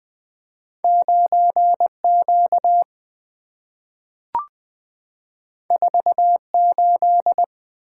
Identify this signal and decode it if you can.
Morse code, heard as a keyed tone, about 20 wpm, 700 Hz. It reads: 9Q 48
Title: Two Letter-Number Combinations - 20wpm